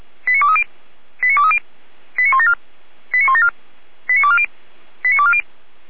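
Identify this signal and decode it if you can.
Five-tone selective call (ZVEI), heard as quick runs of five tones, tones of about 70 ms. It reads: 89230 89230 89174 89174 89240 89240
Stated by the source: ZVEI five-tone selective call received on an SDR, recorded by the Feuerwehr Orlamuende fire brigade.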